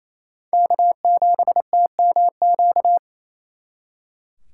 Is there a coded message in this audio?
Yes: Morse code from 0.5 to 3.0 s, heard as a keyed tone, about 35 wpm, 700 Hz. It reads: K7TMQ